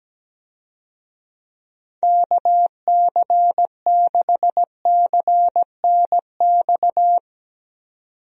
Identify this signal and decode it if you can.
Morse code, heard as a keyed tone, about 17 wpm, 700 Hz. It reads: KC6CNX
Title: USA Call Signs - 6 Characters - 17wpm